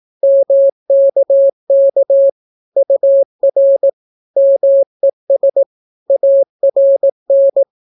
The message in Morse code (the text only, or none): MKK UR MES ARN